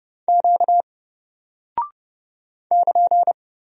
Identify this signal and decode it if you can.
Morse code, heard as a keyed tone, about 32 wpm, 700 Hz. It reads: Q <KN>
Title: Single Character (Letter:Number:Punctuation:Prosign) 30wpm